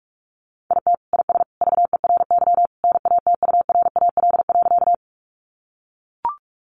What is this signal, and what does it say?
Morse code, heard as a keyed tone, about 44 wpm, 700 Hz. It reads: IT IS VERY NATURAL.
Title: Sentences from Top 700 Words 45wpm